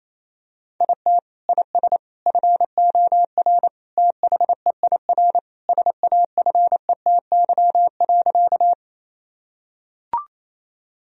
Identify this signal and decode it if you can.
Morse code, heard as a keyed tone, about 28 wpm, 700 Hz. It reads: IT IS FOR THEIR SAFETY.
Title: Sentences from Top 1000 Words - Review Part 4 - 28wpm